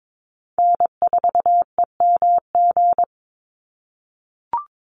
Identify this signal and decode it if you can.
Morse code, heard as a keyed tone, about 22 wpm, 700 Hz. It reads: N4EMG
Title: NA SOTA Chaser Callsigns 22wpm